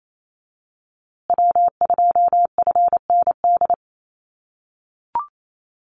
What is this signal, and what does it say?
Morse code, heard as a keyed tone, about 28 wpm, 700 Hz. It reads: W2FND
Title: USA Call Signs Repeated 2x 28wpm